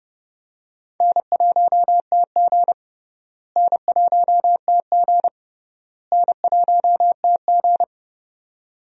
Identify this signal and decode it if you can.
Morse code, heard as a keyed tone, about 32 wpm, 700 Hz. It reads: N1TG N1TG N1TG
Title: USA Call Signs - 4 Characters - Repeated 3x - 30wpm